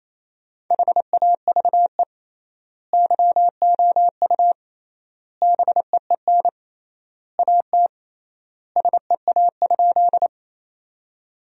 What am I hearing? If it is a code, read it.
Morse code, heard as a keyed tone, about 27 wpm, 700 Hz. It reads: HAVE YOU BEEN AT SEA?
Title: 3x Word-Spacing - Sentences from Top 300 Words - 28wpm